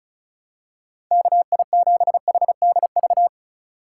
Morse code, heard as a keyed tone, about 34 wpm, 700 Hz. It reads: KI7HDV
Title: USA Call Signs - 6 Characters - 2x Character Spacing - 35wpm